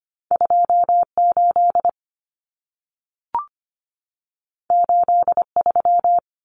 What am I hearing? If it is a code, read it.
Morse code, heard as a keyed tone, about 25 wpm, 700 Hz. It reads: 28 83
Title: Numbers - 2-Digit 25wpm